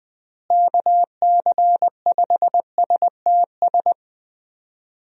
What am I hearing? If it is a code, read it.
Morse code, heard as a keyed tone, about 20 wpm, 700 Hz. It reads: KC5STS
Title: USA Call Signs - 6 Characters - Repeated 3x - 20wpm